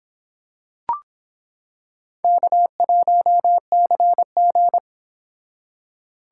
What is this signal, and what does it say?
Morse code, heard as a keyed tone, about 26 wpm, 700 Hz. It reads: K1CG